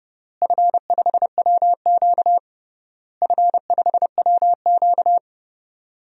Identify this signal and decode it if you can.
Morse code, heard as a keyed tone, about 30 wpm, 700 Hz. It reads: F5WQ F5WQ
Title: DXCC-100 Call Signs - 4 Characters - Repeated 3x - 30wpm